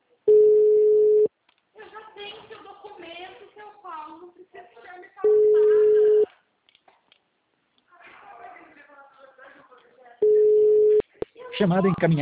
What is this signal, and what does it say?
Ringback tone (Europe, 425 Hz), with speech.